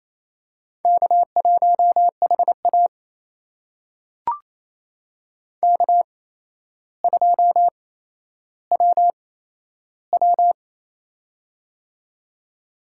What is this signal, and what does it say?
Morse code, heard as a keyed tone, about 28 wpm, spaced as for 12 wpm, 700 Hz. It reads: K1HA K2WW
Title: USA Call Signs - 4 Characters - 8x Character Spacing - 28wpm